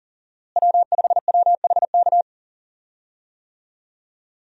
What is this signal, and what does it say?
Morse code, heard as a keyed tone, about 40 wpm, 700 Hz. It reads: W5WHK
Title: USA Call Signs - 5 Characters - Repeated 3x - 40wpm